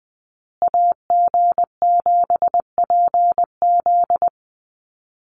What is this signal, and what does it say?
Morse code, heard as a keyed tone, about 20 wpm, 700 Hz. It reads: AG7PZ
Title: USA Call Signs Repeated 2x 20wpm